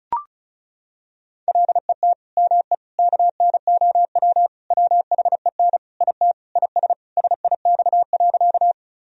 Morse code, heard as a keyed tone, about 36 wpm, 700 Hz. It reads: LET ME KNOW WHEN IT IS SIX.
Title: Sentences from Top 1000 Words - Review Part 2 - 35wpm